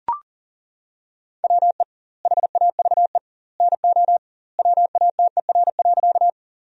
Morse code, heard as a keyed tone, about 37 wpm, 700 Hz. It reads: WE HAVE NO WATER.
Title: 2x Word-Spacing - Sentences from Top 100 Words - 40wpm